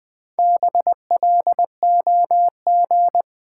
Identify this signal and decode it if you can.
Morse code, heard as a keyed tone, about 20 wpm, 700 Hz. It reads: BLOG
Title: Build A Word - 2 Letters At A Time - 4-Letter Words - 20wpm